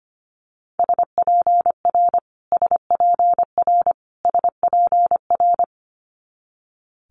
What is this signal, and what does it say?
Morse code, heard as a keyed tone, about 25 wpm, 700 Hz. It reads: SPR SPR SPR
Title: Multi-letter Phonemes Repeated 3x 25wpm